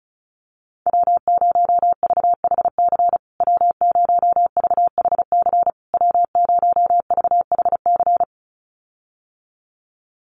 Morse code, heard as a keyed tone, about 34 wpm, 700 Hz. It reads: W0VHC W0VHC W0VHC
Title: USA Call Signs Repeated 3x 35wpm